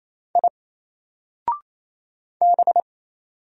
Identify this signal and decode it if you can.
Morse code, heard as a keyed tone, about 28 wpm, 700 Hz. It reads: I B
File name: Single-Letter-Number-28wpm